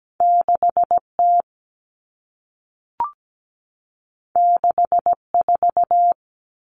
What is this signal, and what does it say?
Morse code, heard as a keyed tone, about 17 wpm, 700 Hz. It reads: 6T 64
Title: Common Cut Numbers - 2-Digit 17wpm